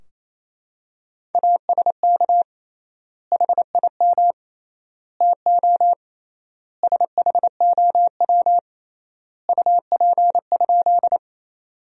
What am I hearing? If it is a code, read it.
Morse code, heard as a keyed tone, about 28 wpm, 700 Hz. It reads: ASK HIM TO SHOW UP?
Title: 3x Word-Spacing - Sentences from Top 200 Words - 28wpm